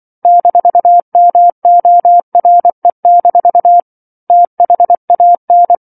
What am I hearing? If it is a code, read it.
Morse code, heard as a keyed tone, about 24 wpm, 700 Hz. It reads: -MORE- THAN
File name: 07-Chapter-07 24 WPM